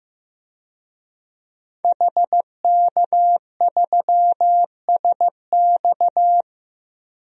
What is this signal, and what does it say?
Morse code, heard as a keyed tone, about 15 wpm, 700 Hz. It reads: HK3SX
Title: DXCC-100 Call Signs - 5 Characters - 15wpm